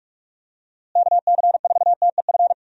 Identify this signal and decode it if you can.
Morse code, heard as a keyed tone, about 46 wpm, 700 Hz. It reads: KC4TEF
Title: USA Call Signs - 6 Characters - Repeated 2x - 45wpm